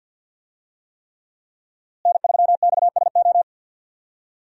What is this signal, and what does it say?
Morse code, heard as a keyed tone, about 52 wpm, 700 Hz. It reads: N3XSO
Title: USA Call Signs - 5 Characters - Repeated 3x - 50wpm